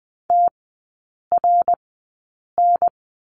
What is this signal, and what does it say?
Morse code, heard as a keyed tone, about 20 wpm, spaced as for 8 wpm, 700 Hz. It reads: TRN
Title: 100 Lesson 17e - Letter u - Three Characters 20wpm